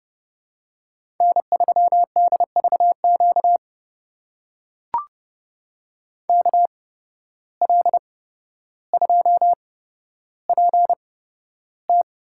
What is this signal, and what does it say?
Morse code, heard as a keyed tone, about 30 wpm, spaced as for 13 wpm, 700 Hz. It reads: N3DVQ KL2PT
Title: USA Call Signs - 5 Characters - 8x Character Spacing - 30wpm